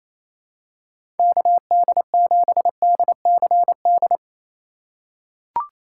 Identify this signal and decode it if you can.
Morse code, heard as a keyed tone, about 28 wpm, 700 Hz. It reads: KD7DCD
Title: USA Call Signs - 6 Characters - 4x Character Spacing - 28wpm